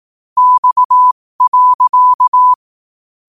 Morse code, heard as a keyed tone, about 18 wpm, spaced as for 15 wpm, 1000 Hz. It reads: X.
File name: qotdfiles / 15wpm